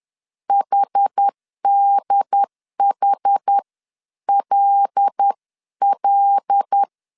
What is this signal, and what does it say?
Morse code, heard as a keyed tone, about 10 wpm, spaced as for 9 wpm, 800 Hz. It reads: HDHLL